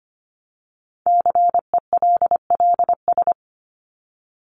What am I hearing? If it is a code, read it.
Morse code, heard as a keyed tone, about 25 wpm, 700 Hz. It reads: CELLS
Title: ES:S Suffix - Top 100 Words with Base Words 25wpm